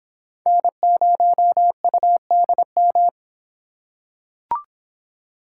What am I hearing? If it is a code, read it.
Morse code, heard as a keyed tone, about 26 wpm, 700 Hz. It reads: N0UDM